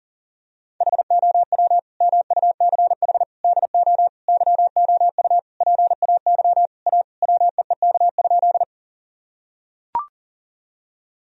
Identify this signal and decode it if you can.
Morse code, heard as a keyed tone, about 40 wpm, 700 Hz. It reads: HOW MUCH DO YOU PAY A WEEK?
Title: Sentences from Top 1000 Words - Review Part 2 - 40wpm